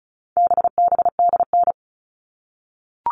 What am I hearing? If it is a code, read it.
Morse code, heard as a keyed tone, about 34 wpm, 700 Hz. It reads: BBDN
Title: All Cut Numbers - 4-Digit 35wpm